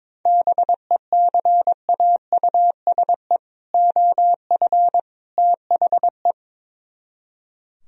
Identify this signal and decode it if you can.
Morse code, heard as a keyed tone, about 22 wpm, 700 Hz. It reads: BECAUSE OF THE
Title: Speed-Racing - Sets of 3 Words 22wpm